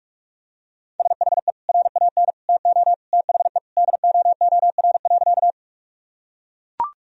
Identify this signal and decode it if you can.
Morse code, heard as a keyed tone, about 43 wpm, 700 Hz. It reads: SHE RAN TO THE DOOR.
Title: Sentences from Top 1000 Words - Review Part 3 - 45wpm